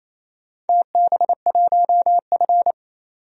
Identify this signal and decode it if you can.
Morse code, heard as a keyed tone, about 28 wpm, 700 Hz. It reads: TB1F
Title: DXCC-100 Call Signs - 4 Characters - 28wpm